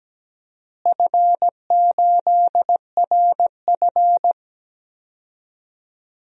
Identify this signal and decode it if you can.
Morse code, heard as a keyed tone, about 17 wpm, 700 Hz. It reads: F8RF
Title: DXCC-100 Call Signs - 4 Characters - Repeated 3x - 17wpm